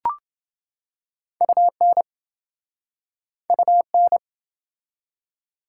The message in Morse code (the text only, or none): UN UN